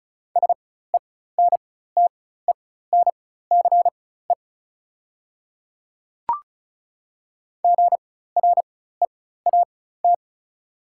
Morse code, heard as a keyed tone, about 36 wpm, spaced as for 16 wpm, 700 Hz. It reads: SENTENCE GREAT
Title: 4x Character Spacing - Top 200 Words - 35wpm